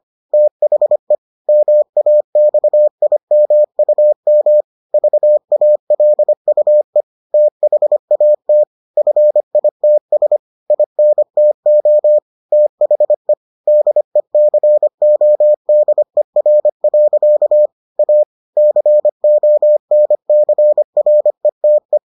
Morse code, heard as a keyed tone, about 25 wpm, 600 Hz. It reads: THE MAXIMUM VALUE THAT FITS INTO THE DECODER. A CONCRETE